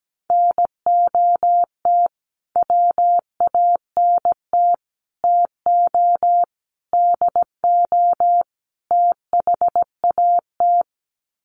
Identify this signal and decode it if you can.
Morse code, heard as a keyed tone, about 17 wpm, 700 Hz. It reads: NOT WANT TO DO THAT
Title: Sets of 5 Words Encore 17wpm